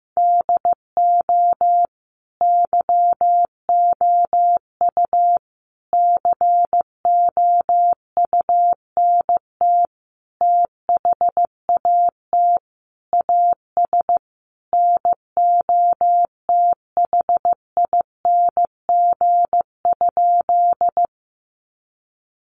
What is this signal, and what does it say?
Morse code, heard as a keyed tone, about 15 wpm, 700 Hz. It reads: DO YOU COUNT THAT AS NOTHING?
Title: Sentences from Top 1500 Words 15wpm